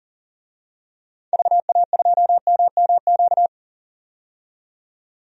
Morse code, heard as a keyed tone, about 38 wpm, 700 Hz. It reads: VA2MMQ
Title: DXCC-100 Call Signs - 6 Characters - 40wpm